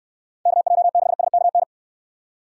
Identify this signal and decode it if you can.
Morse code, heard as a keyed tone, about 65 wpm, 700 Hz. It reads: KJ6SKN